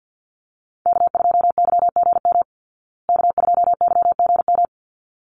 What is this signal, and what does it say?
Morse code, heard as a keyed tone, about 50 wpm, 700 Hz. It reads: K2YGM K2YGM
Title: Call Signs Repeated 3x - 50wpm